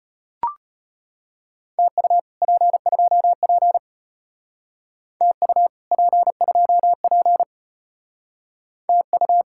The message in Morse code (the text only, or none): TU P2P TU P2P TU